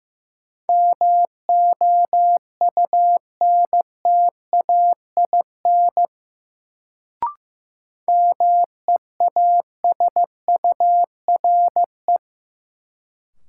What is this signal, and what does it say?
Morse code, heard as a keyed tone, about 16 wpm, 700 Hz. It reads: MOUNTAIN MEASURE
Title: Sentences from Top 1000 Words - Review Part 1 - New Word Drill - 15wpm